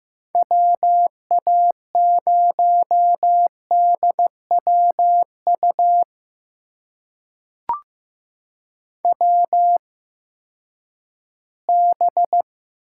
Morse code, heard as a keyed tone, about 15 wpm, spaced as for 10 wpm, 700 Hz. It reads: WA0DWU WB
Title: USA Call Signs - 6 Characters - 8x Character Spacing - 15wpm